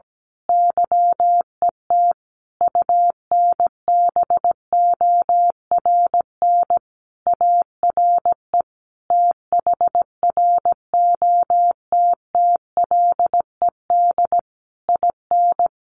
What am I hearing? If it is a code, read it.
Morse code, heard as a keyed tone, about 17 wpm, 700 Hz. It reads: YET UNBORN ARE THROTTLED IN
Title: A Sound Of Thunder 17wpm